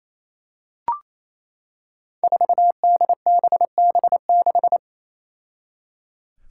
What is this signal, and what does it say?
Morse code, heard as a keyed tone, about 28 wpm, 700 Hz. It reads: 4DBB6
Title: Five Challenging Letter-Number Combinations 28wpm